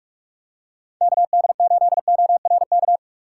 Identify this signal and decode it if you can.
Morse code, heard as a keyed tone, about 40 wpm, 700 Hz. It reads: KD8ORK